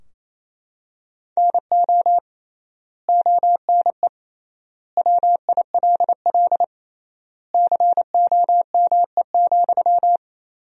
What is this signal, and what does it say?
Morse code, heard as a keyed tone, about 28 wpm, 700 Hz. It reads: NO ONE WILL COME,
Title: 3x Word-Spacing - Sentences from Top 100 Words - 28wpm